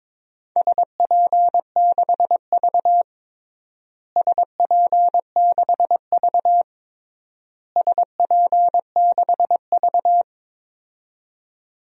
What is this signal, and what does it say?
Morse code, heard as a keyed tone, about 22 wpm, 700 Hz. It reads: SP6V SP6V SP6V
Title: DXCC-100 Call Signs - 4 Characters - Repeated 3x - 22wpm